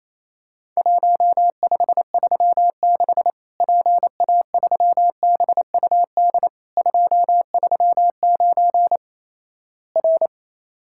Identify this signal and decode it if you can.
Morse code, heard as a keyed tone, about 28 wpm, 700 Hz. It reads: 1536 PA3BUD 239 R
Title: Worked All Europe DX Contest - QTC Exchange - 28wpm